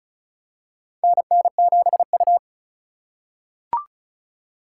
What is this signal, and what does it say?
Morse code, heard as a keyed tone, about 36 wpm, 700 Hz. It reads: NN7U
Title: USA Call Signs - 4 Characters - 3x Character Spacing - 35wpm